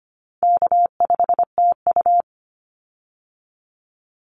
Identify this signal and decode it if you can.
Morse code, heard as a keyed tone, about 25 wpm, 700 Hz. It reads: K5TU
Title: Call Signs Repeated 2x - 25wpm